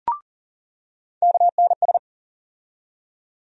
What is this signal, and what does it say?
Morse code, heard as a keyed tone, about 40 wpm, 700 Hz. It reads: KNS
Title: USA Call Sign Postfixes - 3 Letters 40wpm